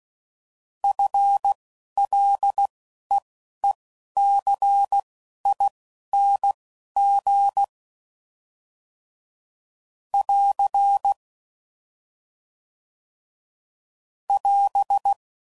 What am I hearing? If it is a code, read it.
Morse code, heard as a keyed tone, about 16 wpm, spaced as for 11 wpm, 800 Hz. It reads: FLEECING <AR> <AS>
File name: Prefixes / FLs & THs / FL Prefixes 1